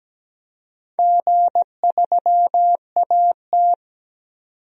Morse code, heard as a keyed tone, about 17 wpm, 700 Hz. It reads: G3AT
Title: DXCC-100 Call Signs - 4 Characters - 17wpm